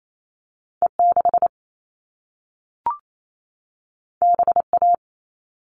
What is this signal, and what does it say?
Morse code, heard as a keyed tone, about 28 wpm, 700 Hz. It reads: E6 BA
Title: All Cut Numbers - 2-Digit 28wpm